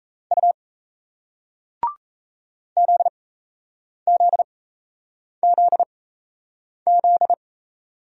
Morse code, heard as a keyed tone, about 38 wpm, spaced as for 8 wpm, 700 Hz. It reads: U ZZZZ